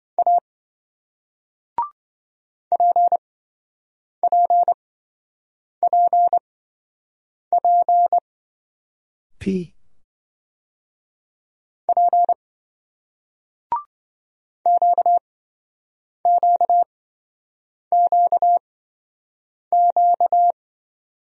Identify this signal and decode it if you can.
Morse code, heard as a keyed tone, about 28 wpm, spaced as for 7 wpm, 700 Hz. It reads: A PPPP P QQQQ